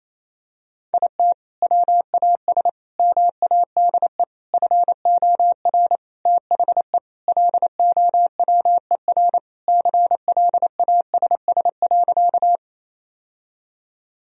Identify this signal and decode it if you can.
Morse code, heard as a keyed tone, about 28 wpm, 700 Hz. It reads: IT WAS MADE FOR THE LOWER CLASS.
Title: Sentences from Top 1000 Words - Review Part 4 - 28wpm